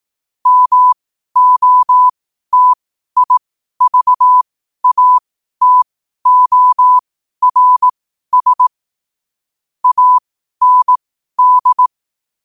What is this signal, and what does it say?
Morse code, heard as a keyed tone, about 18 wpm, spaced as for 12 wpm, 1000 Hz. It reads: MOTIVATORS AND
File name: qotdfiles / 12wpm